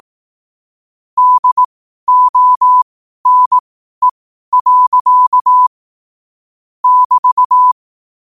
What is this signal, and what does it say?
Morse code, heard as a keyed tone, about 18 wpm, spaced as for 12 wpm, 1000 Hz. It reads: DONE. <BT>
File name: qotdfiles / 12wpm